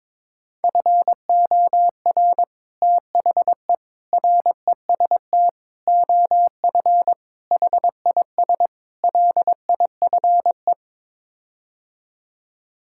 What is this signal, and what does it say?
Morse code, heard as a keyed tone, about 22 wpm, 700 Hz. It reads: FOR THE REST OF HIS LIFE
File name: practice / 22wpm / Sets of 6 Words - 22wpm